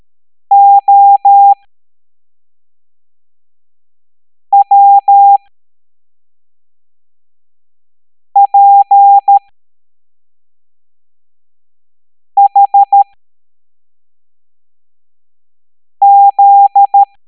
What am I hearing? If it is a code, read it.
Morse code, heard as a keyed tone, about 13 wpm, spaced as for 3 wpm, 800 Hz. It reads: OWPHZ
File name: hamdata / L07-EX02-HQP